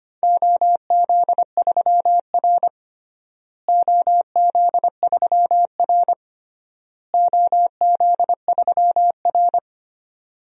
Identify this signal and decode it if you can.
Morse code, heard as a keyed tone, about 25 wpm, 700 Hz. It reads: OZ3R OZ3R OZ3R
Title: DXCC-100 Call Signs - 4 Characters - Repeated 3x - 25wpm